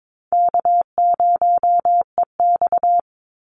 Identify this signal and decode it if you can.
Morse code, heard as a keyed tone, about 22 wpm, 700 Hz. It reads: K0EX
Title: Easy Call Signs - 22wpm